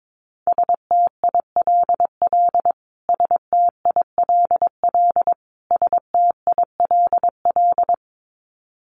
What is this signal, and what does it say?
Morse code, heard as a keyed tone, about 22 wpm, 700 Hz. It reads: STILL STILL STILL
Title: Top 500 Words Repeated 22wpm